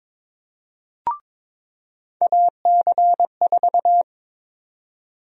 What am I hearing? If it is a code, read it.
Morse code, heard as a keyed tone, about 22 wpm, 700 Hz. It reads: AC4
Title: USA Call Sign Prefixes - 22wpm